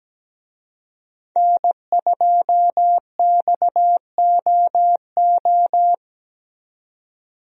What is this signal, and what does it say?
Morse code, heard as a keyed tone, about 17 wpm, 700 Hz. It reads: N2XOO
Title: USA Call Signs - 5 Characters - 17wpm